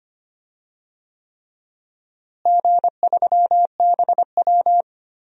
Morse code, heard as a keyed tone, about 25 wpm, 700 Hz. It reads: G3BW
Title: DXCC-100 Call Signs - 4 Characters - Repeated 3x - 25wpm